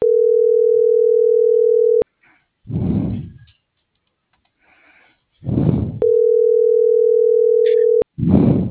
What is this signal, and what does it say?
Ringback tone (North America), 440 + 480 Hz.